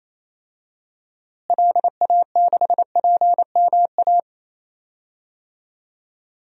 Morse code, heard as a keyed tone, about 28 wpm, 700 Hz. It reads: LA6PMA
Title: DXCC-100 Call Signs - 6 Characters - Repeated 2x - 28wpm